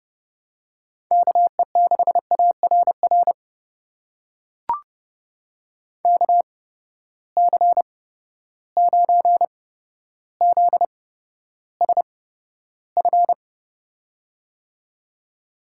Morse code, heard as a keyed tone, about 30 wpm, spaced as for 13 wpm, 700 Hz. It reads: KE6ARR KC9ZSF
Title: USA Call Signs - 6 Characters - 8x Character Spacing - 30wpm